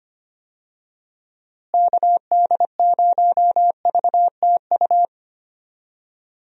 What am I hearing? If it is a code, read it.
Morse code, heard as a keyed tone, about 25 wpm, 700 Hz. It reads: KD0VTU
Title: USA Call Signs - 6 Characters - Repeated 3x - 25wpm